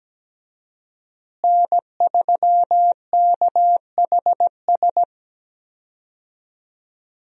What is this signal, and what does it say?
Morse code, heard as a keyed tone, about 17 wpm, 700 Hz. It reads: N3KHS
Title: USA Call Signs - 5 Characters - Repeated 3x - 17wpm